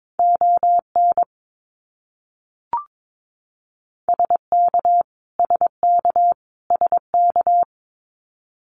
Morse code, heard as a keyed tone, about 22 wpm, 700 Hz. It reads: ON SK SK SK
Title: Canadian Provinces and Territories - Repeated 3x - 22wpm